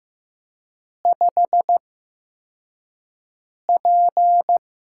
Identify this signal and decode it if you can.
Morse code, heard as a keyed tone, about 15 wpm, spaced as for 4 wpm, 700 Hz. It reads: 5P
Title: USA Call Signs - 8x Character Spacing - 15wpm